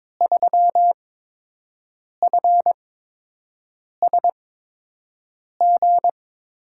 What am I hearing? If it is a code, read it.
Morse code, heard as a keyed tone, about 21 wpm, spaced as for 6 wpm, 700 Hz. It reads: 3FSG
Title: USA Call Signs - 5 Characters - 8x Character Spacing - 22wpm